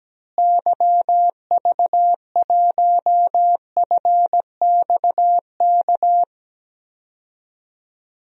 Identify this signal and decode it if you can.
Morse code, heard as a keyed tone, about 17 wpm, 700 Hz. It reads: YV1FXK
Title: DXCC-100 Call Signs - 6 Characters - Repeated 3x - 17wpm